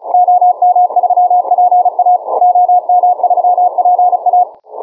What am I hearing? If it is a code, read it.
Morse code, heard as a keyed tone, about 27 wpm, 710 Hz. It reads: OM3PA OM3PA